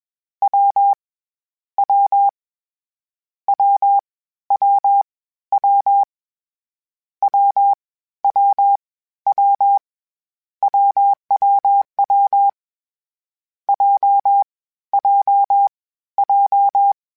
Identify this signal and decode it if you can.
Morse code, heard as a keyed tone, about 21 wpm, 800 Hz. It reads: W W W W W W W W WWW J J J
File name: Difficult Combos / Difficult Combos-mp3s / AWJ1 @20